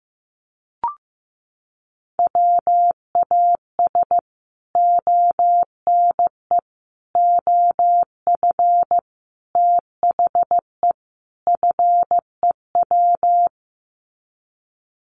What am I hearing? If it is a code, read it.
Morse code, heard as a keyed tone, about 15 wpm, 700 Hz. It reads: WAS ONE OF THE FEW
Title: Sets of 5 Words Encore 15wpm